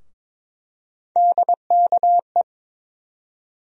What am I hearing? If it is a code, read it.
Morse code, heard as a keyed tone, about 22 wpm, 700 Hz. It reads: DKE
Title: USA Call Sign Postfixes - 3 Letters 22wpm